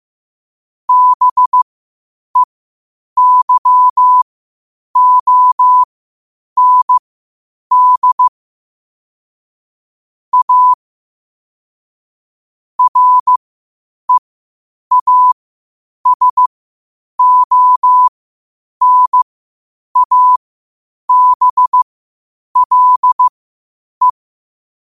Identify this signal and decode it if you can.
Morse code, heard as a keyed tone, about 15 wpm, spaced as for 8 wpm, 1000 Hz. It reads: BEYOND A REASONABLE
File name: qotdfiles / 8wpm